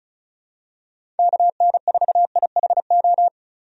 Morse code, heard as a keyed tone, about 35 wpm, 700 Hz. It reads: KN4IHO
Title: USA Call Signs - 6 Characters - 35wpm